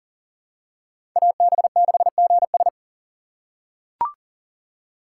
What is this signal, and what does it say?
Morse code, heard as a keyed tone, about 40 wpm, 700 Hz. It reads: AB6GS
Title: USA Call Signs - 5 Characters - 2x Character Spacing - 40wpm